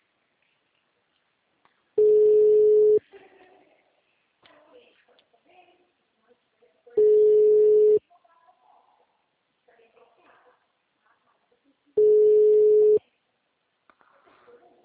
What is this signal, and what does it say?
Ringback tone (Europe, 425 Hz), with speech.
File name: out- -2012- - - .5276